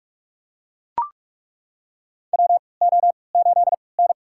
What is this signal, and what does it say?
Morse code, heard as a keyed tone, about 41 wpm, spaced as for 25 wpm, 690 Hz. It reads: WO8N
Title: USA Call Signs - 4 Characters - 3x Character Spacing - 45wpm